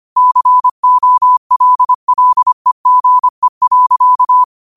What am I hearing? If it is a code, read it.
Morse code, heard as a keyed tone, about 25 wpm, 1000 Hz. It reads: COLLEGE.
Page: Quotes for Sat, 16 Aug 2025 in Morse Code at 25 words per minute.